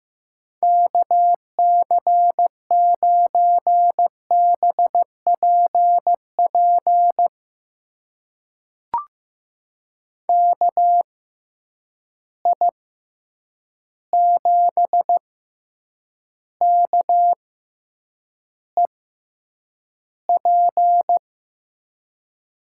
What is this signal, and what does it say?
Morse code, heard as a keyed tone, about 15 wpm, spaced as for 8 wpm, 700 Hz. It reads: KC9BPP KI7KEP
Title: USA Call Signs - 6 Characters - 6x Character Spacing - 15wpm